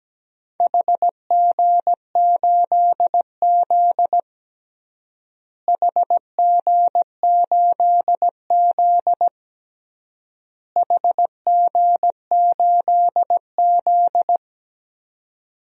Morse code, heard as a keyed tone, about 17 wpm, 700 Hz. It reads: HG8Z HG8Z HG8Z
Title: DXCC-100 Call Signs - 4 Characters - Repeated 3x - 17wpm